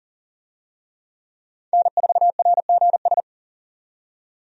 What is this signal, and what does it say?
Morse code, heard as a keyed tone, about 40 wpm, 700 Hz. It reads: N4RGS
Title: USA Call Signs - 5 Characters - 40wpm